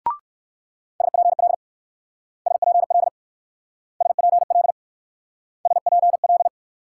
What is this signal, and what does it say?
Morse code, heard as a keyed tone, about 54 wpm, 700 Hz. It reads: SPL SPL SPL SPL
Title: Speed-Racing Multi-letter Phonemes - 45wpm